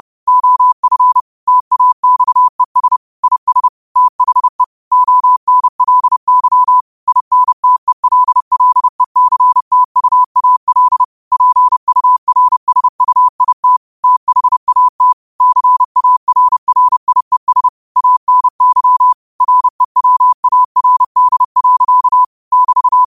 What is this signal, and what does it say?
Morse code, heard as a keyed tone, about 30 wpm, 1000 Hz. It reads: OF TAXES IS THE ONLY INTELLECTUAL PURSUIT THAT CARRIES ANY REWARD. <BT>
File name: qotdfiles / 30wpm